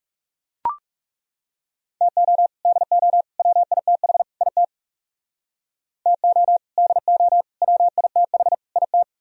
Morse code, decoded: TO DO WITH IT TO DO WITH IT

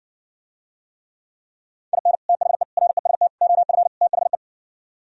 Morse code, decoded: IN THE REST OF THE

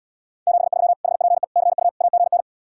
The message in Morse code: 73 HPE CU AGN

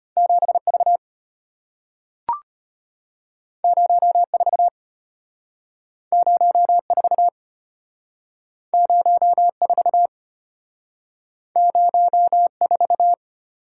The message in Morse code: 7V 04 04 04 04

